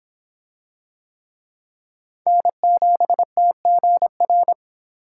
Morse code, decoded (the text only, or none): N7TGR